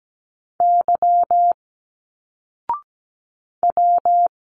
Morse code, heard as a keyed tone, about 17 wpm, 700 Hz. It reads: Y W
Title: Single Letters 17wpm